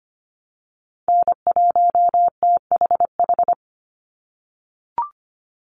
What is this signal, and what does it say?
Morse code, heard as a keyed tone, about 25 wpm, 700 Hz. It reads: N1THH
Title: USA Call Signs 25wpm